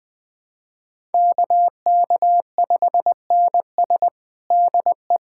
Morse code, heard as a keyed tone, about 21 wpm, 700 Hz. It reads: KK5NS DE
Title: USA Callsign DE Callsign 20wpm